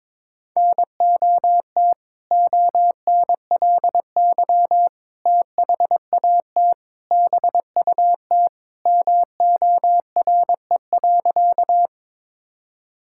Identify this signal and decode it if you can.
Morse code, heard as a keyed tone, about 22 wpm, 700 Hz. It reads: NOT ONLY THAT BUT MORE.
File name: practice / 22wpm / Sentences with 2-Word Phrases - 22wpm